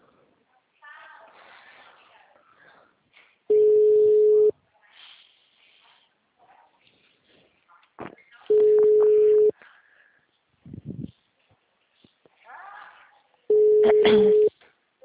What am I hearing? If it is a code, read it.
Ringback tone (Europe, 425 Hz).
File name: out- -2005- - - .58625